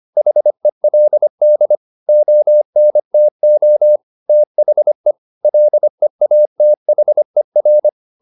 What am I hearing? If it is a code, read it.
Morse code, heard as a keyed tone, about 25 wpm, 600 Hz. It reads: HELD ONTO THE LEATHER
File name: SS123 25